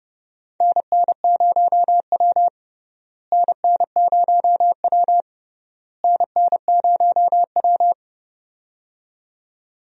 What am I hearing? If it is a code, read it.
Morse code, heard as a keyed tone, about 29 wpm, 700 Hz. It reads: NN0W NN0W NN0W
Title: USA Call Signs - 4 Characters - Repeated 3x - 30wpm